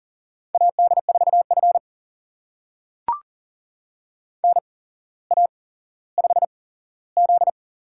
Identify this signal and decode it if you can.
Morse code, heard as a keyed tone, about 39 wpm, spaced as for 17 wpm, 700 Hz. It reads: AD4F NA5Z
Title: USA Call Signs - 4 Characters - 8x Character Spacing - 40wpm